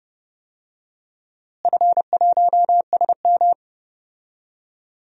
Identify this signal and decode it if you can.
Morse code, heard as a keyed tone, about 32 wpm, 700 Hz. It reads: F1SM
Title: DXCC-100 Call Signs - 4 Characters - Repeated 3x - 30wpm